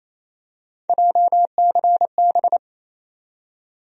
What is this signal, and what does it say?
Morse code, heard as a keyed tone, about 28 wpm, 700 Hz. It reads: JCB
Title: USA Call Sign Postfixes - 3 Letters 28wpm